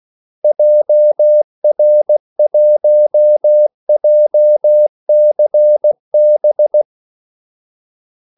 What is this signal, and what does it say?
Morse code, heard as a keyed tone, about 16 wpm, 600 Hz. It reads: JR1JCB
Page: Kaldesignaler 16 wpm